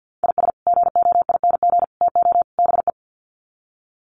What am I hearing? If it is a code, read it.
Morse code, heard as a keyed tone, about 50 wpm, 700 Hz. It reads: IS GOING TO BE
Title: Sets of 4 Words 50wpm